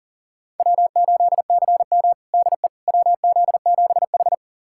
Morse code, heard as a keyed tone, about 40 wpm, 700 Hz. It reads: W8CM DE WZ7H
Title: USA Callsign DE Callsign - 4 Characters - 40wpm